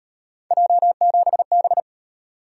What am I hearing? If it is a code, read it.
Morse code, heard as a keyed tone, about 38 wpm, 700 Hz. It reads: J7B